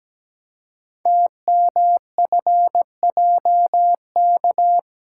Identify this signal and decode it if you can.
Morse code, heard as a keyed tone, about 17 wpm, 700 Hz. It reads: TMFJK
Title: Code Groups - 5 Letter:Numbers - 17wpm